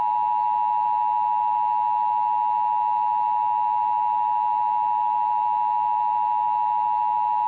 Emergency Alert System (SAME) tones (the 853 and 960 Hz attention tone). It is heard from 0.9 to 6.4 s.